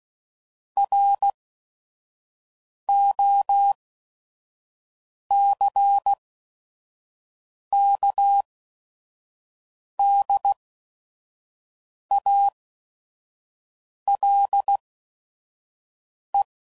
Morse code, heard as a keyed tone, about 16 wpm, spaced as for 5 wpm, 800 Hz. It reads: ROCKDALE
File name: CW Code Groups @5WPM / fromSM's / Cities